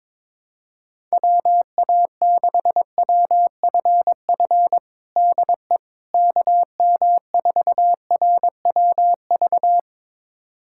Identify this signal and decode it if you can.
Morse code, heard as a keyed tone, about 22 wpm, 700 Hz. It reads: WA6WFF DE KM4RWV
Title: USA Callsign DE Callsign 22wpm